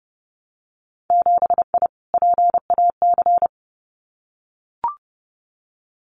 Morse code, heard as a keyed tone, about 30 wpm, 700 Hz. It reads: 7I PAC
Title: Winter Field Day Exchanges Encore 30wpm